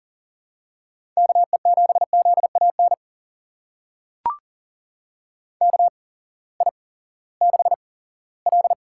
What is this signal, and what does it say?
Morse code, heard as a keyed tone, about 40 wpm, spaced as for 20 wpm, 700 Hz. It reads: KE7ZAN KI6L
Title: USA Call Signs - 6 Characters - 8x Character Spacing - 40wpm